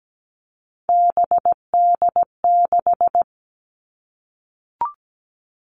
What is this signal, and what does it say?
Morse code, heard as a keyed tone, about 17 wpm, 700 Hz. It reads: BD6
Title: All Cut Numbers - 3-Digit 17wpm